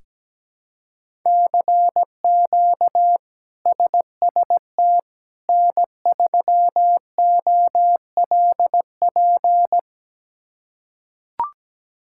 Morse code, heard as a keyed tone, about 17 wpm, 700 Hz. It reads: CQ SST N3OLP
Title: K1USN SST QSO Practice - 17wpm